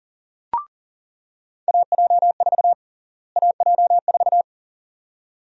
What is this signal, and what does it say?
Morse code, heard as a keyed tone, about 37 wpm, 700 Hz. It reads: AJ4 AJ4
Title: USA Call Sign Prefixes - Repeated 2x - 40wpm